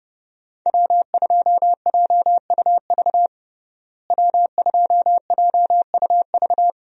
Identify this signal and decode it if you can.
Morse code, heard as a keyed tone, about 30 wpm, 700 Hz. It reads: W2JUV W2JUV
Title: USA Call Signs - 5 Characters - Repeated 2x - 30wpm